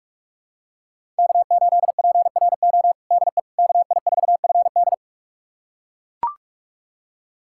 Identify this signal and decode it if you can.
Morse code, heard as a keyed tone, about 41 wpm, 700 Hz. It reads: K8PRO DE KI4FD
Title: USA Callsign DE Callsign - 5 Characters - 45wpm